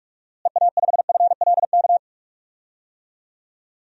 Morse code, heard as a keyed tone, about 46 wpm, 700 Hz. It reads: EA5FLK
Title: DXCC-100 Call Signs - 6 Characters - Repeated 3x - 45wpm